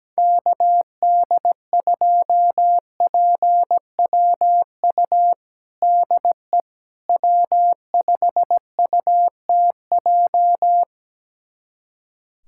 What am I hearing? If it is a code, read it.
Morse code, heard as a keyed tone, about 17 wpm, 700 Hz. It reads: KD2PWU DE W5UTJ
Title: USA Callsign DE Callsign 17wpm